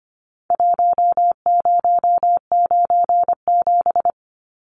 Morse code, heard as a keyed tone, about 25 wpm, 700 Hz. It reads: 1097
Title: Numbers - Contest Style - 4-Digit 25wpm